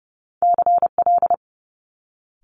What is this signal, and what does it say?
Morse code, heard as a keyed tone, about 30 wpm, 700 Hz. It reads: CL
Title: Multi-letter Phonemes Quick Fire 30wpm